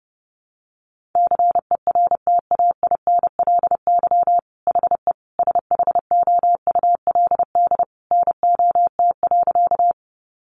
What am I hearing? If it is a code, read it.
Morse code, heard as a keyed tone, about 30 wpm, 700 Hz. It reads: CERTAINLY HE SHOULD NOT.
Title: Sentences from Top 900 Words 30wpm